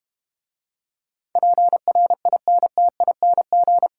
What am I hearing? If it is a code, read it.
Morse code, heard as a keyed tone, about 32 wpm, 700 Hz. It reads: PRINTING